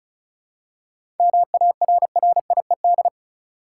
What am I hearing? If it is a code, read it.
Morse code, heard as a keyed tone, about 34 wpm, 700 Hz. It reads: MARRIED
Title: Seven Letter Words - 35wpm